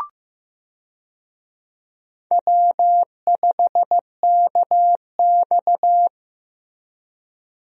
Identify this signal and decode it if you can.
Morse code, heard as a keyed tone, about 15 wpm, 700 Hz. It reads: W5KX
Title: USA Call Signs - 4 Characters - Repeated 3x - 15wpm